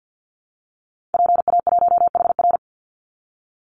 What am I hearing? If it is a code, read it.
Morse code, heard as a keyed tone, about 50 wpm, 700 Hz. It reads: PA1HR
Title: Call Signs Repeated 2x - 50wpm